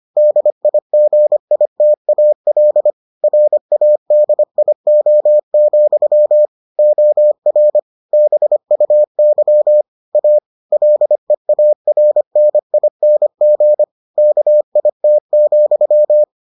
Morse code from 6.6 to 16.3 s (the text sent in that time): OR BUY A LEARNING KIT,